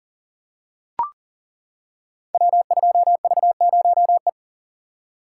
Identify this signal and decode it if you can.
Morse code, heard as a keyed tone, about 42 wpm, 700 Hz. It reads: W2V0E
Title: Code Groups - 5 Letter:Numbers - 40wpm